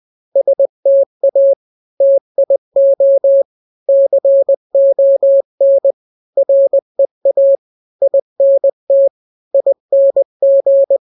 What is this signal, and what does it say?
Morse code, heard as a keyed tone, about 20 wpm, 550 Hz. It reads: STA TIO CON REA INT ING
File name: n-grams / 3-grams 20 18 WPM